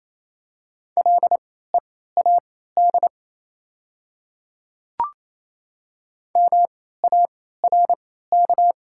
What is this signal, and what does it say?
Morse code, heard as a keyed tone, about 28 wpm, spaced as for 16 wpm, 700 Hz. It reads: LEAD MARK
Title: 3x Character Spacing - Top 500 Words - 28wpm